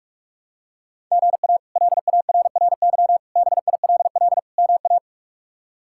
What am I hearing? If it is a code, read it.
Morse code, heard as a keyed tone, about 47 wpm, 700 Hz. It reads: GA LARRY BILL MA
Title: K1USN SST QSO Practice - 45wpm